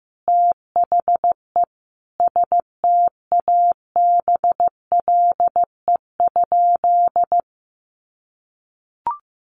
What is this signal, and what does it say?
Morse code, heard as a keyed tone, about 15 wpm, 700 Hz. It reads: THE STABLE?
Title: Sentences from Top 1500 Words 15wpm